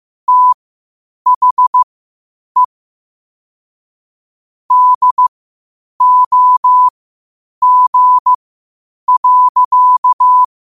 Morse code, heard as a keyed tone, about 15 wpm, spaced as for 8 wpm, 1000 Hz. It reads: THE DOG.
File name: qotdfiles / 8wpm